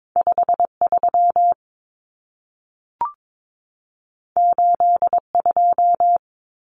Morse code, heard as a keyed tone, about 22 wpm, 700 Hz. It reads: 53 82
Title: Common Cut Numbers - 2-Digit 22wpm